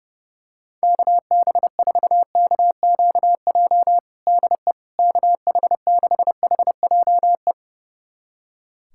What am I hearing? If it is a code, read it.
Morse code, heard as a keyed tone, about 32 wpm, 700 Hz. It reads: KB4KQJ DE KH6HJE
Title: USA Callsign DE Callsign 30wpm